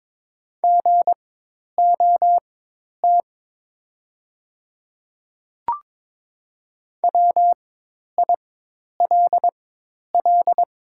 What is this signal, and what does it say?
Morse code, heard as a keyed tone, about 22 wpm, spaced as for 10 wpm, 700 Hz. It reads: GOT WILL
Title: 4x Character Spacing - Top 300 Words - 22wpm